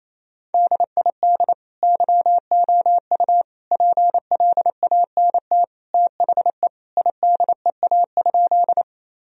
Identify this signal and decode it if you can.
Morse code, heard as a keyed tone, about 28 wpm, 700 Hz. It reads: DID YOU PLANT THE IDEA?
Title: Sentences from Top 1000 Words - Review Part 1 - 28wpm